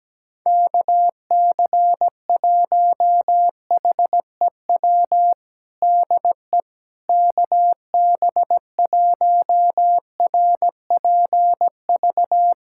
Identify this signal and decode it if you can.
Morse code, heard as a keyed tone, about 17 wpm, 700 Hz. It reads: KC1HEW DE KB1RPV
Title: USA Callsign DE Callsign - 6 Characters - 17wpm